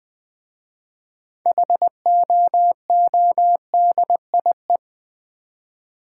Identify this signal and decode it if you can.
Morse code, heard as a keyed tone, about 20 wpm, 700 Hz. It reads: HOODIE
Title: Difficult Words with Hints - 20wpm